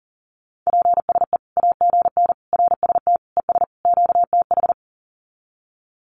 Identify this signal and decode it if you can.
Morse code, heard as a keyed tone, about 40 wpm, 700 Hz. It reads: PSE AGN RST ES QTH
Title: Common QSO Phrases 40wpm